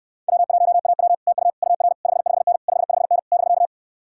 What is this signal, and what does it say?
Morse code, heard as a keyed tone, about 66 wpm, 700 Hz. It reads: W1AW TU UR 55N 55N <BK>